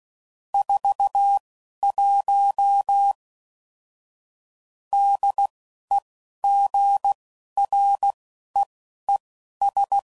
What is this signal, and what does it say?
Morse code, heard as a keyed tone, about 16 wpm, spaced as for 11 wpm, 800 Hz. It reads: 41 DEGREES
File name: CW Code Groups @10WPM / Weather Sets / Weather 4 14